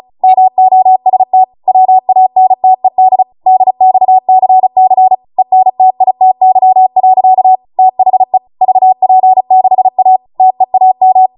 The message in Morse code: MOST WANTED DXCC ENTITY. THE VP6A TEAM